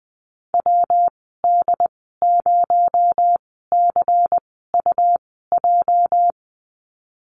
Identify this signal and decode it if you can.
Morse code, heard as a keyed tone, about 20 wpm, spaced as for 15 wpm, 700 Hz. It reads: WD0CUJ
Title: 238 Lesson 38g - Letter q - Call Signs 20wpm